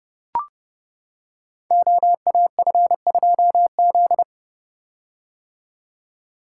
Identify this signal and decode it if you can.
Morse code, heard as a keyed tone, about 30 wpm, 700 Hz. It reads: OAF2Z
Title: Code Groups - 5 Letter:Numbers - 30wpm